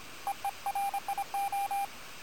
Morse code, heard as a keyed tone, about 26 wpm, 800 Hz, with radio noise. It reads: EERIO
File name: attachments / QSO HA9PP